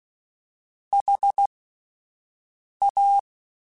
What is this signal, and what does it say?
Morse code, heard as a keyed tone, about 16 wpm, spaced as for 5 wpm, 800 Hz. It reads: HA
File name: Alphabetical / Alphabetical mp3s / Hs @5